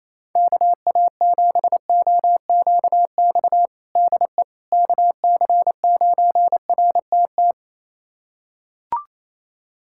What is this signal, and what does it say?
Morse code, heard as a keyed tone, about 28 wpm, 700 Hz. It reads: KA7OQX DE KC9RTT
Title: USA Callsign DE Callsign - 6 Characters - 28wpm